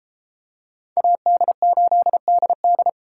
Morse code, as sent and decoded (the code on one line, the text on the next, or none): .- -.. ---.. -.. -..
AD8DD